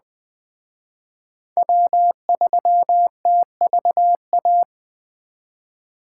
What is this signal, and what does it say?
Morse code, heard as a keyed tone, about 20 wpm, 700 Hz. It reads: W3TVA
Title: Speed Racing - USA 1x3 Call Signs 17wpm